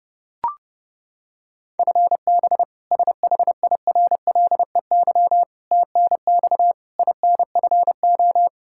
Morse code, heard as a keyed tone, about 31 wpm, 700 Hz. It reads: FB SHIRLEY TNX INFO
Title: K1USN SST QSO Practice 30wpm